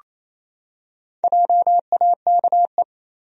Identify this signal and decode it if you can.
Morse code, heard as a keyed tone, about 28 wpm, 700 Hz. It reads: JAKE
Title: USA Common Names - Max 4 Letters - 28wpm